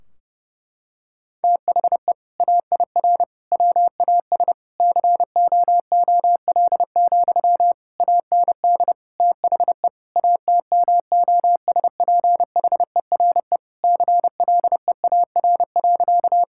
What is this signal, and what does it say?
Morse code, heard as a keyed tone, about 30 wpm, 700 Hz. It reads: THE AIR WAS COOL, AND THE ATMOSPHERE CLEAR.
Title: Sentences from Top 1600 Words 30wpm